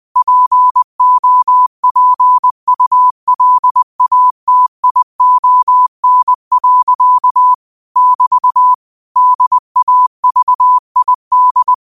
Morse code, heard as a keyed tone, about 20 wpm, 1000 Hz. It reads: POPULATION. <BT> DAVID